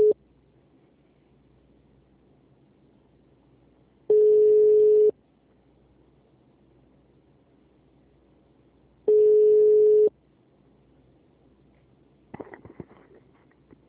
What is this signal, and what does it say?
Ringback tone (Europe, 425 Hz).